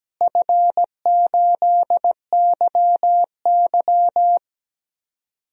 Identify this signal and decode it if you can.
Morse code, heard as a keyed tone, about 17 wpm, 700 Hz. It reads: F8YY
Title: DXCC-100 Call Signs - 4 Characters - Repeated 3x - 17wpm